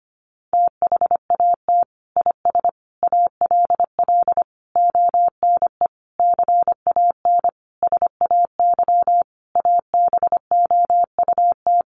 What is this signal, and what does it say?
Morse code, heard as a keyed tone, about 25 wpm, 700 Hz. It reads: THAT IS ALL ONE CAN SAY ABOUT
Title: Sentences from Top 200 Words 25wpm